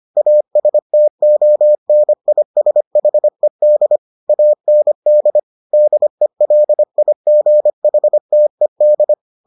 Morse code, heard as a keyed tone, about 25 wpm, 600 Hz. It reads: ASTONISHED AND DELIGHTED